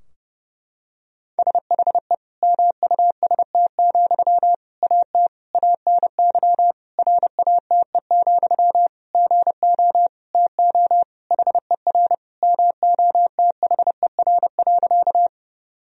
Morse code, heard as a keyed tone, about 30 wpm, 700 Hz. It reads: SHE MUST, AT ANY RATE, GO TO HER MOTHER.
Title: Sentences from Top 1000 Words - Review Part 4 - 30wpm